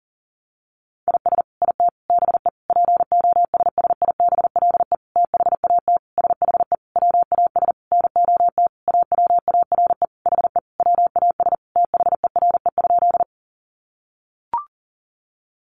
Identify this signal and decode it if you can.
Morse code, heard as a keyed tone, about 40 wpm, 700 Hz. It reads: IS IT BE POSSIBLE THAT SHE WAS NOT AWARE HE WAS THERE?
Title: Sentences from Top 1200 Words 40wpm